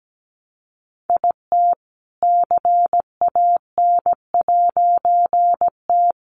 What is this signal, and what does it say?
Morse code, heard as a keyed tone, about 17 wpm, 700 Hz. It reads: IT CAN'T
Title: A Sound Of Thunder 17wpm